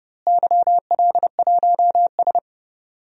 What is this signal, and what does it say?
Morse code, heard as a keyed tone, about 30 wpm, 700 Hz. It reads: YL1S
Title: DXCC-100 Call Signs - 4 Characters - Repeated 2x - 30wpm